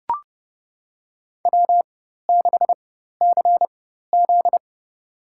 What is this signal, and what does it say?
Morse code, heard as a keyed tone, about 32 wpm, spaced as for 14 wpm, 700 Hz. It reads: W6CZ
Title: USA Call Signs - 4 Characters - 4x Character Spacing - 30wpm